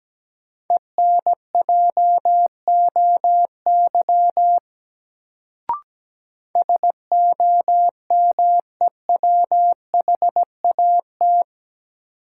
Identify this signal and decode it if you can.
Morse code, heard as a keyed tone, about 17 wpm, 700 Hz. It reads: ENJOY SOMEWHAT
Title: Sentences from Top 1300 Words - New Word Drill Alt - 17wpm